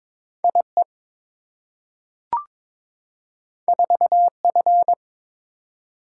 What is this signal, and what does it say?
Morse code, heard as a keyed tone, about 22 wpm, 700 Hz. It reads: IE 4F
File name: practice / 22wpm / Two Letter-Number Combinations - 22wpm